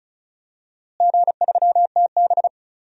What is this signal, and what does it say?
Morse code, heard as a keyed tone, about 36 wpm, 700 Hz. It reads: G3TB
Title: DXCC-100 Call Signs - 4 Characters - Repeated 2x - 35wpm